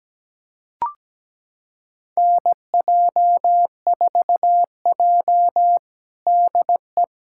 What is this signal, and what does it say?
Morse code, heard as a keyed tone, about 17 wpm, 700 Hz. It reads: NJ4J DE
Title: USA Callsign DE Callsign - 4 Characters - 17wpm